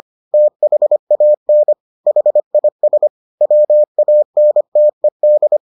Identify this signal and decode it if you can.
Morse code, heard as a keyed tone, about 25 wpm, 600 Hz. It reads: THAN HIS WANTED